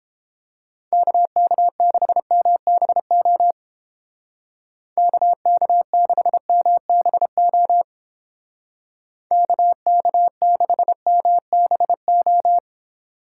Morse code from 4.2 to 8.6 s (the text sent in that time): KK6MBO